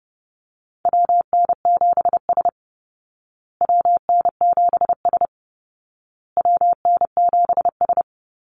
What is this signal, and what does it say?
Morse code, heard as a keyed tone, about 30 wpm, 700 Hz. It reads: WN7S WN7S WN7S
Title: Call Signs Repeated 2x - 30wpm